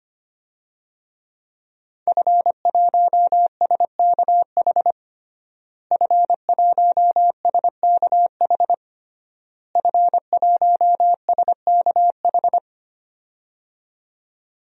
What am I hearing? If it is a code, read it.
Morse code, heard as a keyed tone, about 25 wpm, 700 Hz. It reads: F1SKH F1SKH F1SKH
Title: DXCC-100 Call Signs - 5 Characters - Repeated 3x - 25wpm